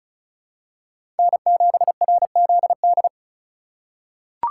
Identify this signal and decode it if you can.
Morse code, heard as a keyed tone, about 36 wpm, 700 Hz. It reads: N7RZD
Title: USA Call Signs - 5 Characters - 2x Character Spacing - 35wpm